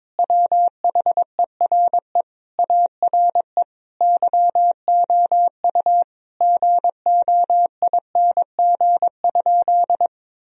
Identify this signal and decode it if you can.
Morse code, heard as a keyed tone, about 22 wpm, 700 Hz. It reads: WHERE ARE YOU GOING?
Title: Sentences from Top 1000 Words - Review Part 2 - 22wpm